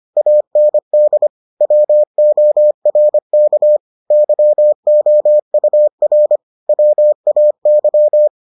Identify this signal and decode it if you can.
Morse code, heard as a keyed tone, about 25 wpm, 600 Hz. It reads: AND WORK YOUR WAY